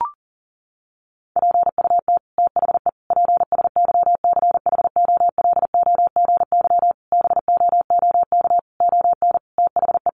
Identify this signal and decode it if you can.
Morse code, heard as a keyed tone, about 40 wpm, 700 Hz. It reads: PUT THE PSYCHOLOGY BOOK ON THE
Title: Sentences from Top 1400 Words 40wpm